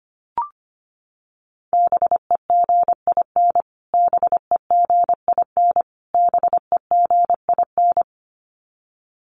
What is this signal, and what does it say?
Morse code, heard as a keyed tone, about 25 wpm, 700 Hz. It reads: BEGIN BEGIN BEGIN
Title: Top 300 Words Repeated 25wpm